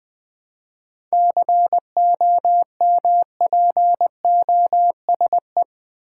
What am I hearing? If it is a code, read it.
Morse code, heard as a keyed tone, about 20 wpm, 700 Hz. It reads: COMPOSE
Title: 7 Letter Words - First 3 Letters Given - 20wpm